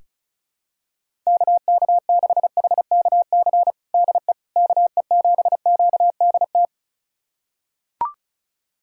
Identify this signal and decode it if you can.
Morse code, heard as a keyed tone, about 34 wpm, 700 Hz. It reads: KK6HKC DE KE7QDT
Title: USA Callsign DE Callsign - 6 Characters - 35wpm